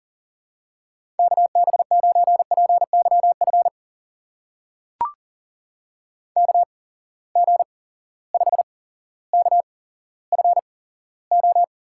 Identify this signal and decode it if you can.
Morse code, heard as a keyed tone, about 40 wpm, spaced as for 17 wpm, 700 Hz. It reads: KB9PYF KG5KFO
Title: USA Call Signs - 6 Characters - 8x Character Spacing - 40wpm